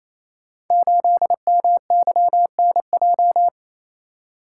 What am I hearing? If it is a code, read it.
Morse code, heard as a keyed tone, about 28 wpm, 700 Hz. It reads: 8MYNJ
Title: Code Groups - 5 Letter:Numbers - 28wpm